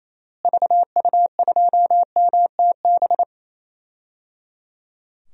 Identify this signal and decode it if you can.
Morse code, heard as a keyed tone, about 28 wpm, 700 Hz. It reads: VU2MTB
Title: DXCC-100 Call Signs - 6 Characters - Repeated 3x - 28wpm